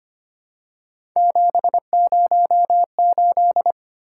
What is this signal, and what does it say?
Morse code, heard as a keyed tone, about 25 wpm, 700 Hz. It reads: 708
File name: Three Letter-Number Combinations - 25wpm